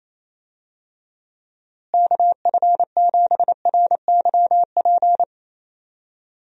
Morse code, heard as a keyed tone, about 27 wpm, 700 Hz. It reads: KF7RYP
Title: USA Call Signs - 6 Characters - Repeated 3x - 28wpm